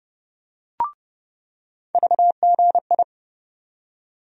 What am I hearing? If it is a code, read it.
Morse code, heard as a keyed tone, about 30 wpm, 700 Hz. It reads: VGI
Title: USA Call Sign Postfixes - 3 Letters 30wpm